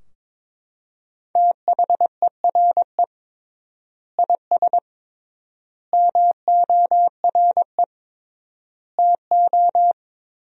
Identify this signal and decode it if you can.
Morse code, heard as a keyed tone, about 22 wpm, 700 Hz. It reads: THERE IS MORE TO
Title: 3x Word-Spacing - Sentences from Top 200 Words - 22wpm